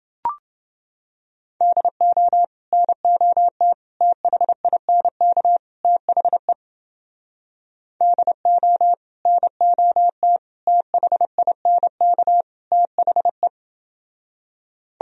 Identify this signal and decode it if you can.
Morse code, heard as a keyed tone, about 29 wpm, 700 Hz. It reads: DO NOT THINK THE DO NOT THINK THE